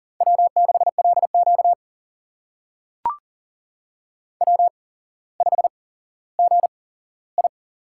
Morse code, heard as a keyed tone, about 40 wpm, spaced as for 17 wpm, 700 Hz. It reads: W6LQ W5GI
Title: USA Call Signs - 4 Characters - 8x Character Spacing - 40wpm